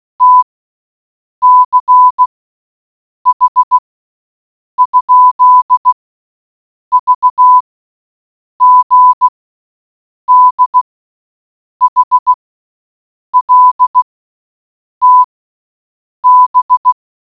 Morse code, heard as a keyed tone, about 16 wpm, 1000 Hz. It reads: T C H ? V G D H L T B